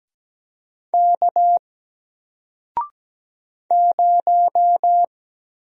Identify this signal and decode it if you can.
Morse code, heard as a keyed tone, about 17 wpm, 700 Hz. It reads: K 0
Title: Single Letter-Number Alt 17wpm